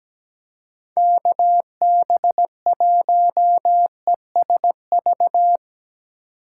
Morse code, heard as a keyed tone, about 19 wpm, 700 Hz. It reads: KB1ESV
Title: Speed Racing - USA 2x3 Call Signs 17wpm